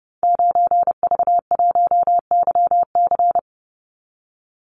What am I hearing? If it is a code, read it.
Morse code, heard as a keyed tone, about 30 wpm, 700 Hz. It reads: 9V1YC
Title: Call Signs - 30wpm